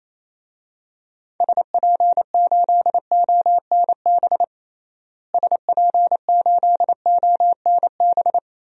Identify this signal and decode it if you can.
Morse code, heard as a keyed tone, about 28 wpm, 700 Hz. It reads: SP8ONB SP8ONB
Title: DXCC-100 Call Signs - 6 Characters - Repeated 3x - 28wpm